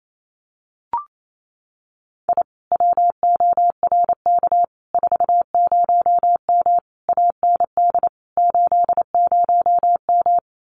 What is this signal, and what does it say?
Morse code, heard as a keyed tone, about 28 wpm, 700 Hz. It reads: I WORK 40M AND 80M
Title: Common QSO Phrases Encore 28wpm